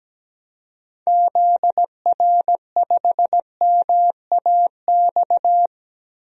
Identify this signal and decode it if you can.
Morse code, heard as a keyed tone, about 17 wpm, 700 Hz. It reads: ZR5MAX
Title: DXCC-100 Call Signs - 6 Characters - Repeated 3x - 17wpm